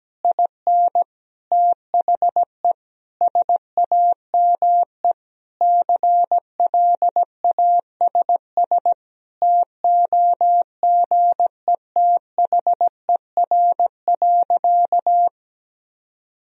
Morse code, heard as a keyed tone, about 17 wpm, 700 Hz. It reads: IN THE SAME CLASS TOGETHER.
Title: Sentences from Top 1000 Words - Review Part 2 - 17wpm